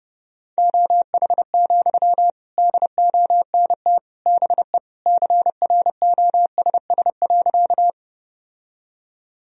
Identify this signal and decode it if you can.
Morse code, heard as a keyed tone, about 30 wpm, 700 Hz. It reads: OH, DONT BE CROSS.
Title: Sentences from Top 1000 Words - Review Part 1 - 30wpm